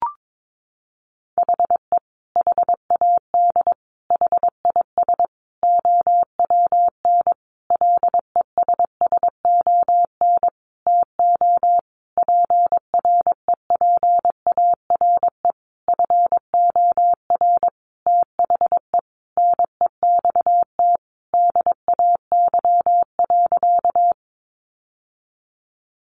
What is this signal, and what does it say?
Morse code, heard as a keyed tone, about 22 wpm, 700 Hz. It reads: HE HAD HIS OWN LESSON TO PREPARE FOR THE NEXT DAY.
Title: Sentences from Top 1200 Words 22wpm